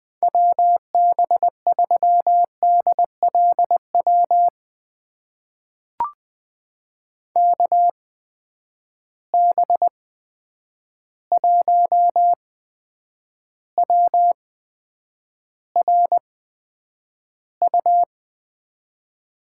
Morse code, heard as a keyed tone, about 20 wpm, spaced as for 9 wpm, 700 Hz. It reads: WB3DLW KB1WRU
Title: USA Call Signs - 6 Characters - 8x Character Spacing - 20wpm